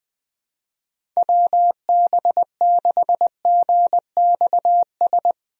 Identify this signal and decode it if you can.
Morse code, heard as a keyed tone, about 21 wpm, 700 Hz. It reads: WB6GXS
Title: USA Call Signs - 6 Characters - 20wpm